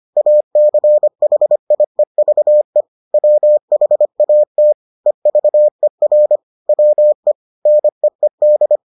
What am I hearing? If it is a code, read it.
Morse code, heard as a keyed tone, about 25 wpm, 600 Hz. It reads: ACHIEVE WHATEVER WE NEED